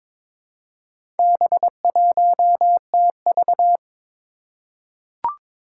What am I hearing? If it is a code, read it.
Morse code, heard as a keyed tone, about 23 wpm, 700 Hz. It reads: B1TV
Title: Four Challenging Letter-Number Combinations 22wpm